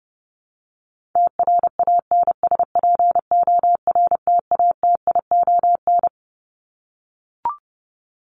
Morse code, heard as a keyed tone, about 30 wpm, 700 Hz. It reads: TRANSPORTATION
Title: Top 1501-1750 Words 30wpm